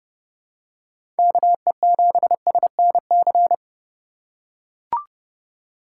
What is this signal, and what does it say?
Morse code, heard as a keyed tone, about 31 wpm, 700 Hz. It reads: KE7SNC
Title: USA Call Signs - 3x Character Spacing - 30wpm